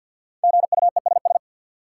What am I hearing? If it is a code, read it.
Morse code, heard as a keyed tone, about 51 wpm, 700 Hz. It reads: GUESS
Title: Sentences from Top 1500 Words - New Word Drill - 50wpm